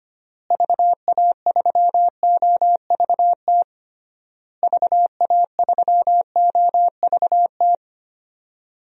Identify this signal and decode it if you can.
Morse code, heard as a keyed tone, about 25 wpm, 700 Hz. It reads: VA3OVT VA3OVT
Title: DXCC-100 Call Signs - 6 Characters - Repeated 3x - 25wpm